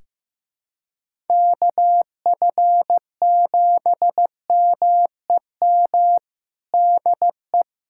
Morse code, heard as a keyed tone, about 16 wpm, 700 Hz. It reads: KF7MEM DE
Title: USA Callsign DE Callsign - 6 Characters - 15wpm